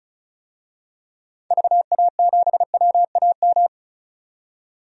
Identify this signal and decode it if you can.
Morse code, heard as a keyed tone, about 36 wpm, 700 Hz. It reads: VA7WAM
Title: DXCC-100 Call Signs - 6 Characters - 35wpm